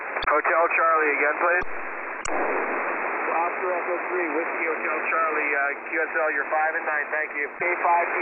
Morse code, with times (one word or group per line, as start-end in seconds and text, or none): none